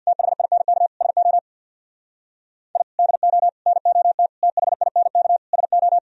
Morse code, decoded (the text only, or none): THINK SO I DO NOT THINK SO